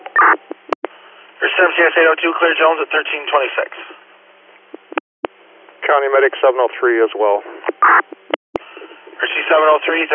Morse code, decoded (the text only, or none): none